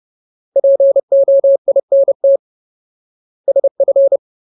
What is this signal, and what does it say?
Morse code, heard as a keyed tone, about 30 wpm, 550 Hz. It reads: POINT SF
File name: Ultimate Word List / 30WPM Ultimate